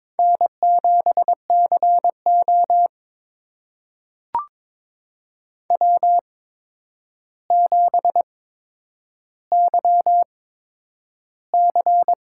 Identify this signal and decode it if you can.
Morse code, heard as a keyed tone, about 22 wpm, spaced as for 9 wpm, 700 Hz. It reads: N7CO W7YC
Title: USA Call Signs - 4 Characters - 8x Character Spacing - 22wpm